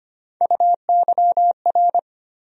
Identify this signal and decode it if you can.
Morse code, heard as a keyed tone, about 25 wpm, 700 Hz. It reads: UYR
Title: USA Call Sign Postfixes - 3 Letters 25wpm